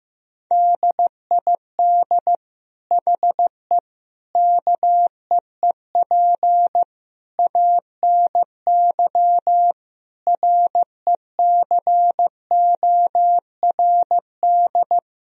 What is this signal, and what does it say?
Morse code, heard as a keyed tone, about 15 wpm, 700 Hz. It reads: DID HE KEEP ANY RECORD
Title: Sentences from Top 1000 Words - Review Part 3 - 15wpm